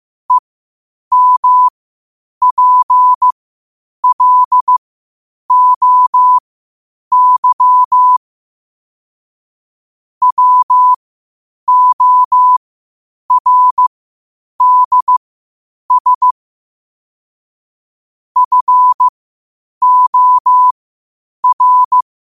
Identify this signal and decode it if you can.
Morse code, heard as a keyed tone, about 15 wpm, spaced as for 8 wpm, 1000 Hz. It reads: EMPLOY WORDS FOR
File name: qotdfiles / 8wpm